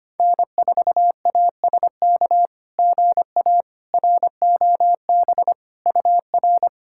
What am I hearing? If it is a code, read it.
Morse code, heard as a keyed tone, about 25 wpm, 700 Hz. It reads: N4ASK GA ROB UR
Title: SOTA QSO Practice 25wpm